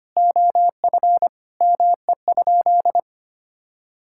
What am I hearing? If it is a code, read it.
Morse code, heard as a keyed tone, about 25 wpm, 700 Hz. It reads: OF ME?
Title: Sentences from Top 1000 Words - Review Part 5 - 25wpm